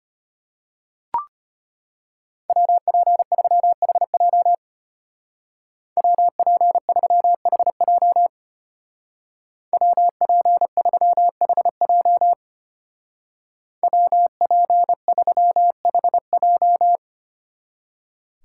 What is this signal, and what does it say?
Morse code, heard as a keyed tone, about 31 wpm, 700 Hz. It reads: WP3HJ WP3HJ WP3HJ WP3HJ